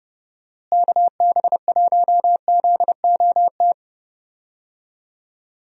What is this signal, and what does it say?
Morse code, heard as a keyed tone, about 30 wpm, 700 Hz. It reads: KB1ZOT
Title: USA Call Signs - 6 Characters - Repeated 3x - 30wpm